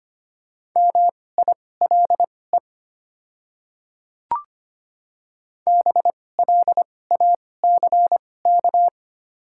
Morse code, heard as a keyed tone, about 25 wpm, spaced as for 18 wpm, 700 Hz. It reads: MILE BLACK
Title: 2x Character Spacing - Top 500 Words - 25wpm